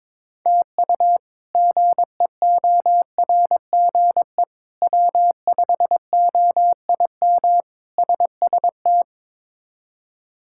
Morse code, heard as a keyed tone, about 22 wpm, 700 Hz. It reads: TU GEORGE W5OIM SST
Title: K1USN SST QSO Practice - 22wpm